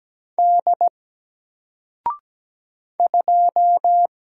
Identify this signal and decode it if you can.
Morse code, heard as a keyed tone, about 17 wpm, 700 Hz. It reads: D 2
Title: Single Letter-Number Alt 17wpm